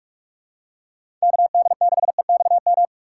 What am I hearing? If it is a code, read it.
Morse code, heard as a keyed tone, about 43 wpm, 700 Hz. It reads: KD6EXM
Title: USA Call Signs - 6 Characters - 3x Character Spacing - 45wpm